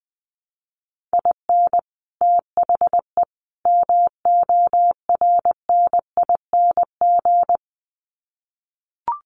Morse code, heard as a keyed tone, about 20 wpm, 700 Hz. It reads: IN THE MORNING
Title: Sets of 3 Words 20wpm